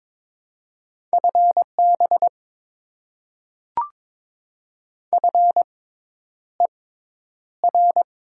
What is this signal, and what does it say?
Morse code, heard as a keyed tone, about 22 wpm, spaced as for 10 wpm, 700 Hz. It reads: FB FER
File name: Top 90 QSO Elements - 6x Character Spacing - 22wpm